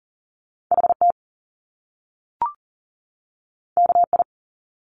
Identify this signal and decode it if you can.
Morse code, heard as a keyed tone, about 39 wpm, 700 Hz. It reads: HT KI
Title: Top Two Letter Combinations in Words 40wpm